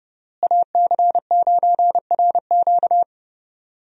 Morse code, heard as a keyed tone, about 30 wpm, 700 Hz. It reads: AC9RQ
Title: USA Call Signs - 5 Characters - Repeated 2x - 30wpm